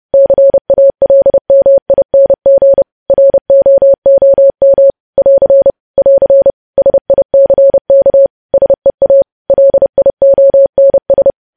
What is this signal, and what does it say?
Morse code, heard as a keyed tone, about 30 wpm, 550 Hz. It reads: CALMING ROOM <AR> <AR> SICK SEA LIONS
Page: news headlines in morse code at 30 wpm